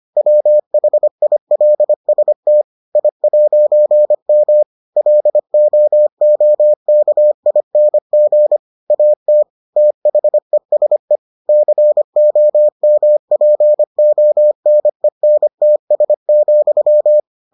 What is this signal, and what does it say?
Morse code, heard as a keyed tone, about 25 wpm, 600 Hz. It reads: WHILST I'M LOOKING AT THESE COMPONENTS,